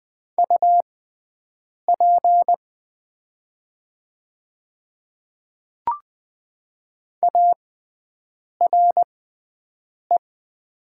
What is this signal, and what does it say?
Morse code, heard as a keyed tone, about 20 wpm, spaced as for 7 wpm, 700 Hz. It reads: UP ARE
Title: 6x Character Spacing - Top 300 Words - 20wpm